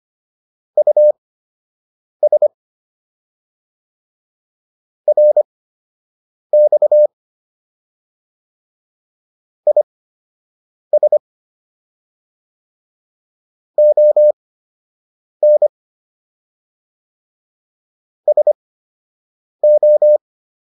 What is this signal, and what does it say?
Morse code, heard as a keyed tone, about 25 wpm, spaced as for 7 wpm, 600 Hz. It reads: US RX IS ON SO